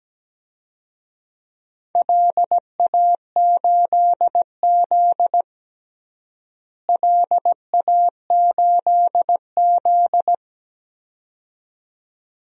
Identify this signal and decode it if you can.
Morse code, heard as a keyed tone, about 17 wpm, 700 Hz. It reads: LA8Z LA8Z
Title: DXCC-100 Call Signs - 4 Characters - Repeated 2x - 17wpm